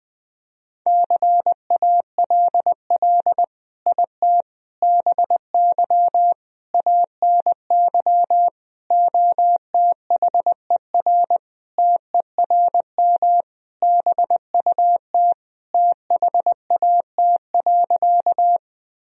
Morse code, heard as a keyed tone, about 20 wpm, 700 Hz. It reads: CALL IT BY ANY OTHER TERM BUT THAT.
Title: Sentences from Top 1000 Words - Review Part 4 - 20wpm